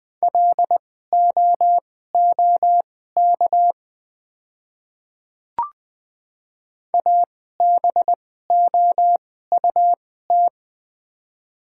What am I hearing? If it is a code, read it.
Morse code, heard as a keyed tone, about 20 wpm, spaced as for 14 wpm, 700 Hz. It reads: LOOK ABOUT
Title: 2x Character Spacing - Top 100 Words - 20wpm